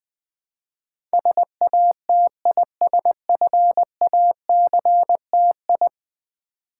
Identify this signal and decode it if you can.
Morse code, heard as a keyed tone, about 20 wpm, 700 Hz. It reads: SATISFACTI
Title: Build A Word - 5 Letters At A Time - 12-Letter Words - 20wpm